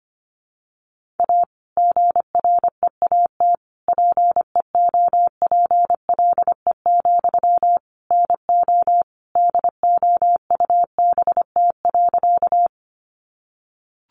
Morse code, heard as a keyed tone, about 25 wpm, 700 Hz. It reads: A GREAT PEOPLE, NO DOUBT.
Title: Sentences from Top 1500 Words 25wpm